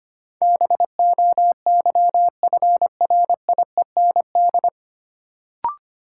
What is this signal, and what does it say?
Morse code, heard as a keyed tone, about 25 wpm, 700 Hz. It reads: BOYFRIEND
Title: Sentences from Top 1000 Words - Review Part 5 - New Word Drill - 25wpm